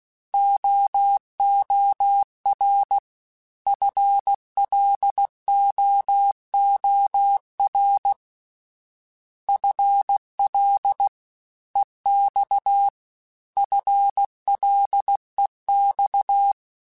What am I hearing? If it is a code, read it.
Morse code, heard as a keyed tone, about 16 wpm, 800 Hz. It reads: OOR FLOOR FL EX FLEX
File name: FL / FL mp3s / Fast FL @15